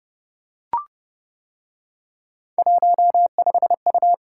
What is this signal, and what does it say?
Morse code, heard as a keyed tone, about 31 wpm, 700 Hz. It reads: 15U